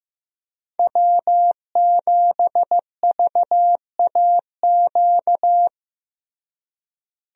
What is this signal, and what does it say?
Morse code, heard as a keyed tone, about 15 wpm, 700 Hz. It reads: W7VAQ
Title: USA Call Signs - 5 Characters - Repeated 2x - 15wpm